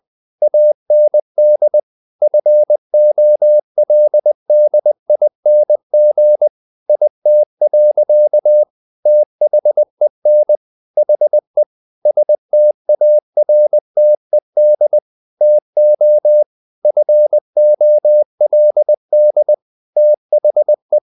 Morse code, heard as a keyed tone, about 20 wpm, 600 Hz. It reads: AND FOLDING IT. THEN HE STARTED TO FOLD THE
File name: the old man and the sea / the old man and the sea part 1 20wpm